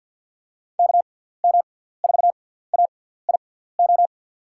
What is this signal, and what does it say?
Morse code, heard as a keyed tone, about 48 wpm, spaced as for 17 wpm, 700 Hz. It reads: KM4AIO
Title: USA Call Signs - 6x Character Spacing - 50wpm